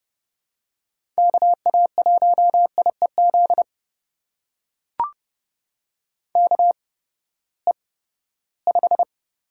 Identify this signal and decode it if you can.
Morse code, heard as a keyed tone, about 29 wpm, spaced as for 17 wpm, 700 Hz. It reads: KA1IEZ KE5
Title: USA Call Signs - 6 Characters - 8x Character Spacing - 30wpm